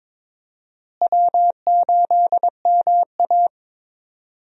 Morse code, heard as a keyed tone, about 22 wpm, 700 Hz. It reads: W8MA
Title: USA Call Signs - 4 Characters - Repeated 2x - 22wpm